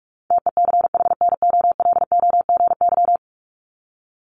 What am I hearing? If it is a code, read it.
Morse code, heard as a keyed tone, about 44 wpm, 700 Hz. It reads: TECHNOLOGY
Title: Top 501-750 Words 45wpm